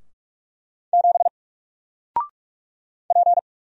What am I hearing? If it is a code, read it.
Morse code, heard as a keyed tone, about 44 wpm, 700 Hz. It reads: 7 P
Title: Single Character (Letter:Number:Punctuation:Prosign) 45wpm